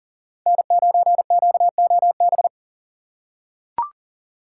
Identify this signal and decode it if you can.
Morse code, heard as a keyed tone, about 40 wpm, 700 Hz. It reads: N9QOB
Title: USA Call Signs - 2x Character Spacing - 40wpm